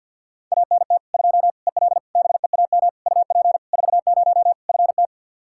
Morse code, heard as a keyed tone, about 53 wpm, 700 Hz. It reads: ANT 2 EL BEAM UP 40 FT
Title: K1USN SST QSO Practice 50wpm